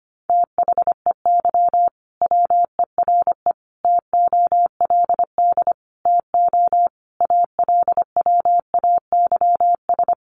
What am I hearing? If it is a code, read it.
Morse code, heard as a keyed tone, about 25 wpm, 700 Hz. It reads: THEY WERE TOLD TO ALWAYS
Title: Sentences from Top 1300 Words 25wpm